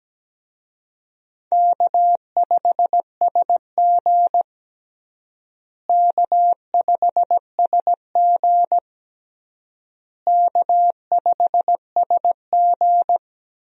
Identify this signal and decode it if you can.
Morse code, heard as a keyed tone, about 17 wpm, 700 Hz. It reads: K5SG K5SG K5SG
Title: USA Call Signs - 4 Characters - Repeated 3x - 17wpm